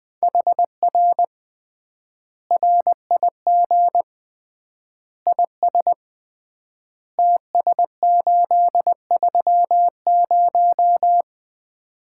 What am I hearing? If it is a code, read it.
Morse code, heard as a keyed tone, about 21 wpm, 700 Hz. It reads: HR RIG IS TS830
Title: Common QSO Phrases - 3x Word-Spacing - 20wpm